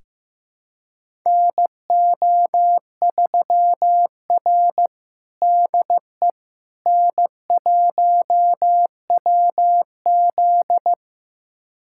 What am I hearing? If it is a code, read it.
Morse code, heard as a keyed tone, about 15 wpm, 700 Hz. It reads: NO3R DE N1WZ
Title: USA Callsign DE Callsign - 4 Characters - 15wpm